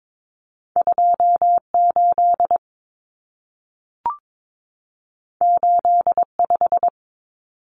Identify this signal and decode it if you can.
Morse code, heard as a keyed tone, about 22 wpm, 700 Hz. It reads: 28 85
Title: Numbers - 2-Digit 22wpm